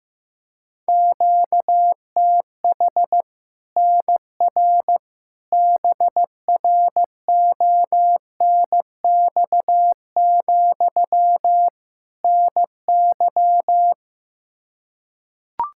Morse code, heard as a keyed tone, about 15 wpm, 700 Hz. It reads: QTH NR BRONX, NY
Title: Practice QSO - Exchange 1 - Beginning - 15wpm